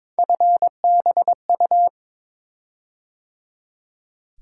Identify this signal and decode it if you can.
Morse code, heard as a keyed tone, about 22 wpm, 700 Hz. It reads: FBU
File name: practice / 22wpm / Three Letter-Number Combinations - 22wpm